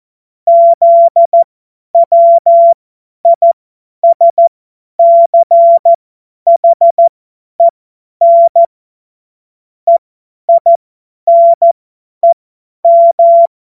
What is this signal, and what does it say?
Morse code, heard as a keyed tone, about 14 wpm, spaced as for 10 wpm, 680 Hz. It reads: ZWISCHEN EINEM
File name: Morsekurs / Witze / Witz 31. Mai 2022 50 F70